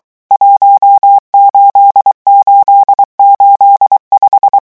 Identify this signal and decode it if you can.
Morse code, heard as a keyed tone, about 23 wpm, 800 Hz. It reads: 18885
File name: morse mp3 / revision11 chiffres vitesse 22